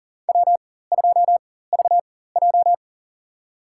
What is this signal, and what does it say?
Morse code, heard as a keyed tone, about 40 wpm, spaced as for 19 wpm, 700 Hz. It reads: W2VJ
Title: USA Call Signs - 4 Characters - 4x Character Spacing - 40wpm